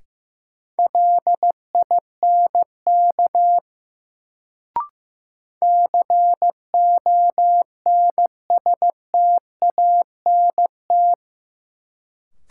Morse code, heard as a keyed tone, about 15 wpm, 700 Hz. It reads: LINK CONSTANT
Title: Sentences from Top 1500 Words - New Word Drill Alt - 15wpm